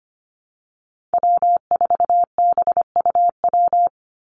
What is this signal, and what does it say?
Morse code, heard as a keyed tone, about 25 wpm, 700 Hz. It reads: W4BUW
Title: Call Signs - 25wpm